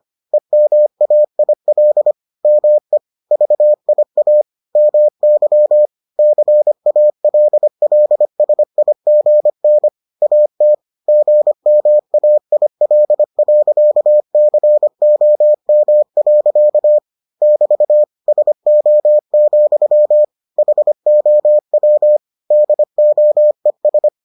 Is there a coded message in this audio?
Yes: Morse code from 0.3 to 24.1 s, heard as a keyed tone, about 25 wpm, 600 Hz. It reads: EMAIL ME VIA MY CALLSIGN AT GMAIL.COM. <BT> SO, HOW DOES